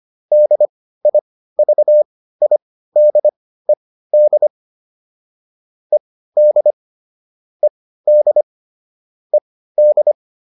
Morse code, heard as a keyed tone, about 25 wpm, spaced as for 15 wpm, 600 Hz. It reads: DIVIDED ED ED ED